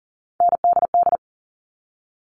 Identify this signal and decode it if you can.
Morse code, heard as a keyed tone, about 37 wpm, 700 Hz. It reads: NDD
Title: All Cut Numbers - 3-Digit 40wpm